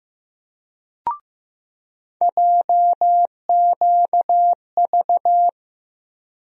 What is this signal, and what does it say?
Morse code, heard as a keyed tone, about 15 wpm, 700 Hz. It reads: JQV
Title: USA Call Sign Postfixes - 3 Letters 15wpm